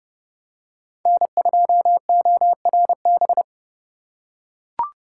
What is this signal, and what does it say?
Morse code, heard as a keyed tone, about 28 wpm, 700 Hz. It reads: N2ORB